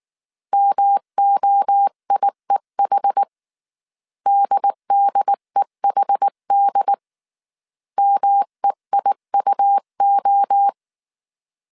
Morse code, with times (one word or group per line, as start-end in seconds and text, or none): 0.5-3.2 MOIEH
4.3-6.9 DDEHD
8.0-10.7 MEIUO